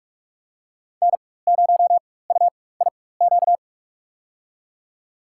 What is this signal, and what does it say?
Morse code, heard as a keyed tone, about 47 wpm, spaced as for 21 wpm, 700 Hz. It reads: N0UIQ
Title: USA Call Signs - 5 Characters - 4x Character Spacing - 45wpm